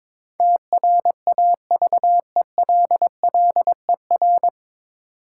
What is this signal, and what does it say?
Morse code, heard as a keyed tone, about 22 wpm, 700 Hz. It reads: TRAVELLER
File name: 9 Letter Words - First 3 Letters Given - 22wpm